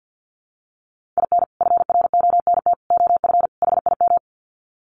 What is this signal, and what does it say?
Morse code, heard as a keyed tone, about 50 wpm, 700 Hz. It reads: IN FRONT OF HIM
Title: Sets of 4 Words 50wpm